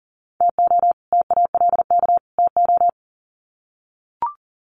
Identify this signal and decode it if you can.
Morse code, heard as a keyed tone, about 40 wpm, 700 Hz. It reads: TO TALK TO
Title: Sets of 3 Words 40wpm